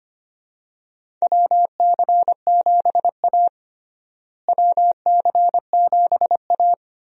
Morse code, heard as a keyed tone, about 25 wpm, 700 Hz. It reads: WC7A WC7A
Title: USA Call Signs - 4 Characters - Repeated 2x - 25wpm